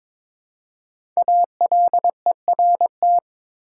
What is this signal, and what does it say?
Morse code, heard as a keyed tone, about 22 wpm, 690 Hz. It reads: ALERT
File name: 5 Letter Words - First 3 Letters Given - 22wpm